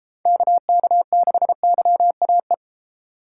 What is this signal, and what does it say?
Morse code, heard as a keyed tone, about 36 wpm, 700 Hz. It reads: KK6YAE